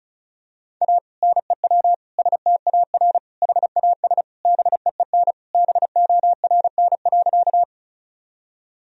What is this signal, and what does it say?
Morse code, heard as a keyed tone, about 36 wpm, 700 Hz. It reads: A NEW STAR HAS BEEN BORN.
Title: Sentences from Top 1000 Words - Review Part 2 - 35wpm